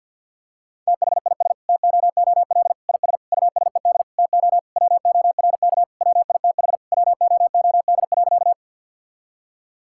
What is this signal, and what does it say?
Morse code, heard as a keyed tone, about 50 wpm, 700 Hz. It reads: THIS TOOL IS USED TO WORK WITH WOOD.
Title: Sentences from Top 1000 Words - Review Part 6 - 50wpm